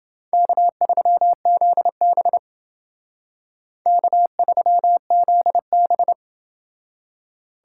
Morse code, heard as a keyed tone, about 28 wpm, 700 Hz. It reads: K3ZB K3ZB